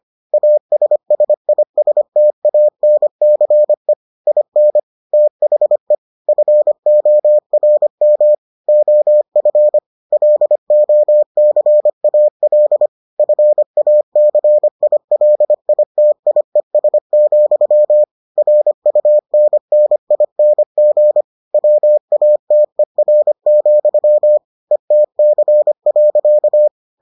Morse code, heard as a keyed tone, about 25 wpm, 600 Hz. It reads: ASSISTANCE IN THE FORM OF LOCAL FACILITIES, RUNNING WATER, ETC.